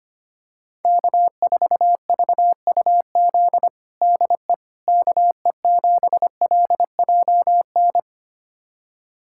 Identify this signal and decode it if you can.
Morse code, heard as a keyed tone, about 25 wpm, 700 Hz. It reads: K4VUZ DE KE7LJN
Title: USA Callsign DE Callsign 25wpm